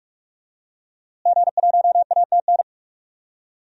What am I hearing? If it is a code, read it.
Morse code, heard as a keyed tone, about 45 wpm, 700 Hz. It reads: G1ATN
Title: DXCC-100 Call Signs - 5 Characters - Repeated 3x - 45wpm